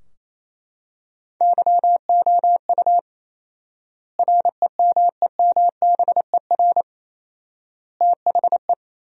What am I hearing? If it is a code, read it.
Morse code, heard as a keyed tone, about 28 wpm, 700 Hz. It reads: YOU REMEMBER THE
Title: 4x Word-Spacing - Sentences from Top 400 Words - 28wpm